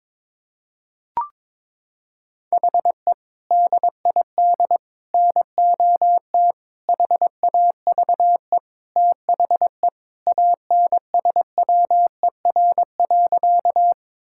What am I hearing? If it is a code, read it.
Morse code, heard as a keyed tone, about 22 wpm, 700 Hz. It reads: HE DID NOT HAVE THE ANSWER.
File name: practice / 22wpm / Sentences with 4-Word Phrases - 22wpm